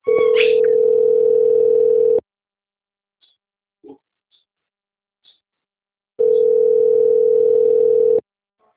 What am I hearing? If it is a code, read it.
Ringback tone (North America), 440 + 480 Hz.
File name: out- - - - - .15295